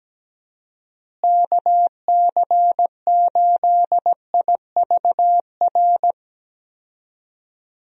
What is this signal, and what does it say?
Morse code, heard as a keyed tone, about 17 wpm, 700 Hz. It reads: KC8IVR
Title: USA Call Signs - 6 Characters - Repeated 2x - 17wpm